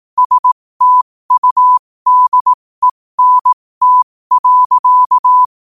Morse code, heard as a keyed tone, about 18 wpm, spaced as for 15 wpm, 1000 Hz. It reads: STUDENT.
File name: qotdfiles / 15wpm